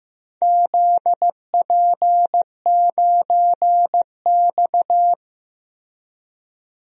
Morse code, heard as a keyed tone, about 15 wpm, 700 Hz. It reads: ZP9X
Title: DXCC-100 Call Signs - 4 Characters - Repeated 3x - 15wpm